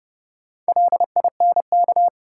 Morse code, heard as a keyed tone, about 30 wpm, 700 Hz. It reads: LINK
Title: Sentences from Top 1500 Words - New Word Drill Alt - 30wpm